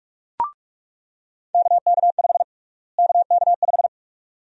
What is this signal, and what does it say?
Morse code, heard as a keyed tone, about 46 wpm, 700 Hz. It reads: KK5 KK5
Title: USA Call Sign Prefixes - Repeated 2x - 45wpm